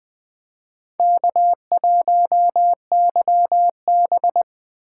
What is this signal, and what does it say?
Morse code, heard as a keyed tone, about 24 wpm, 700 Hz. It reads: K1YB
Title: Speed Racing - USA Call Signs 20wpm